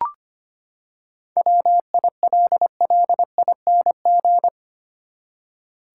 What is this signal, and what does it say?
Morse code, heard as a keyed tone, about 25 wpm, 700 Hz. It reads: WILLING
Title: Seven Letter Words - 25wpm